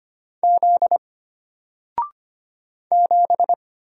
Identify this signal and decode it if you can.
Morse code, heard as a keyed tone, about 25 wpm, 700 Hz. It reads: Z 7
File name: Single-Letter-Number-25wpm